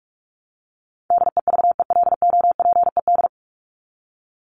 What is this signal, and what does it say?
Morse code, heard as a keyed tone, about 44 wpm, 700 Hz. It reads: DEVELOPED
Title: ED Suffix - Top 100 Words with Base Words 45wpm